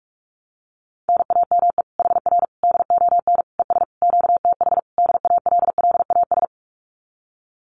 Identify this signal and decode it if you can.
Morse code, heard as a keyed tone, about 46 wpm, 700 Hz. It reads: NAME HR DON ES QTH DALLAS
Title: Common QSO Phrases Encore 45wpm